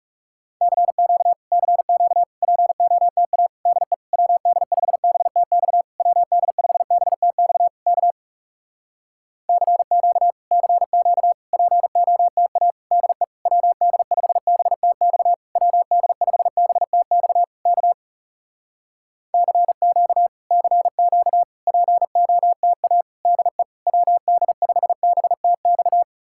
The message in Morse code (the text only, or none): CQ CQ POTA DE WD5BTX WD5BTX K CQ CQ POTA DE WD5BTX WD5BTX K CQ CQ POTA DE WD5BTX